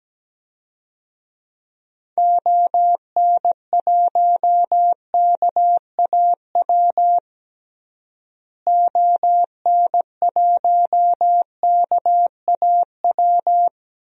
Morse code, heard as a keyed tone, about 17 wpm, 700 Hz. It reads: ON1KAW ON1KAW
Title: DXCC-100 Call Signs - 6 Characters - Repeated 3x - 17wpm